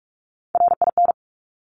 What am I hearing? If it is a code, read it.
Morse code, heard as a keyed tone, about 46 wpm, 700 Hz. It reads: RIN
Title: Top Three Letter Combinations in Words 45wpm